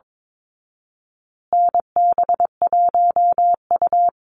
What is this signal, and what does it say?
Morse code, heard as a keyed tone, about 22 wpm, 700 Hz. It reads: NB1U
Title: Call Signs Repeated 3x - 22wpm